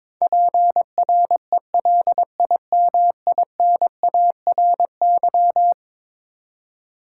Morse code, heard as a keyed tone, about 22 wpm, 700 Hz. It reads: PRELIMINARY
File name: practice / 22wpm / Eleven Letter Words - 22wpm